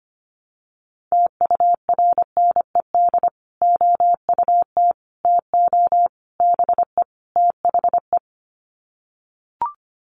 Morse code, heard as a keyed tone, about 25 wpm, 700 Hz. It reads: TURNED OUT TO BE THE
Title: Sets of 5 Words Encore 25wpm